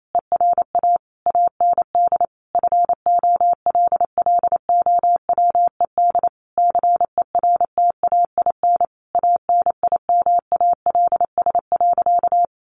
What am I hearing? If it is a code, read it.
Morse code, heard as a keyed tone, about 28 wpm, 700 Hz. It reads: ERA AND FOLLOWED CERTAIN ANIMALS.
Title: A Sound Of Thunder 28wpm